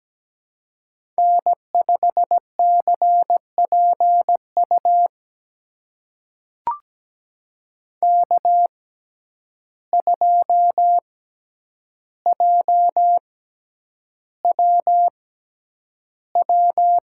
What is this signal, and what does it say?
Morse code, heard as a keyed tone, about 17 wpm, spaced as for 9 wpm, 700 Hz. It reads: N5CPU K2JWW
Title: USA Call Signs - 5 Characters - 6x Character Spacing - 17wpm